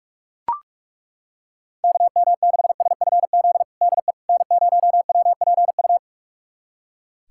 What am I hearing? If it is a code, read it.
Morse code, heard as a keyed tone, about 43 wpm, 700 Hz. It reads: KM6SFZ DE N0WPU
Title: USA Callsign DE Callsign 45wpm